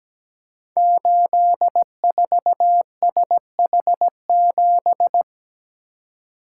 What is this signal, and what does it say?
Morse code, heard as a keyed tone, about 17 wpm, 700 Hz. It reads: 84SH7
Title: Code Groups - 5 Letter:Numbers - 17wpm